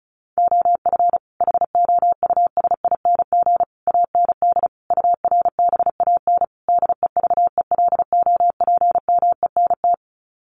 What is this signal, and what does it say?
Morse code, heard as a keyed tone, about 34 wpm, 700 Hz. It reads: OF HOUSING AND URBAN DEVELOPMENT
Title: Sets of 5 Words Encore2 35wpm